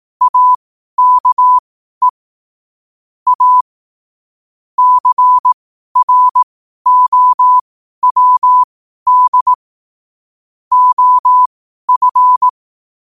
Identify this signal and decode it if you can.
Morse code, heard as a keyed tone, about 18 wpm, spaced as for 12 wpm, 1000 Hz. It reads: AKE A CROWD OF
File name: qotdfiles / 12wpm